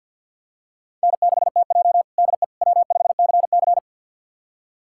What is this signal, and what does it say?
Morse code, heard as a keyed tone, about 50 wpm, 700 Hz. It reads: N6TJ DE W5CC
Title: USA Callsign DE Callsign - 4 Characters - 50wpm